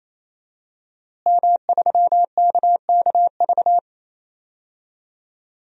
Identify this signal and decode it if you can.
Morse code, heard as a keyed tone, about 28 wpm, 700 Hz. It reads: M3KKV
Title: DXCC-100 Call Signs - 5 Characters - Repeated 3x - 28wpm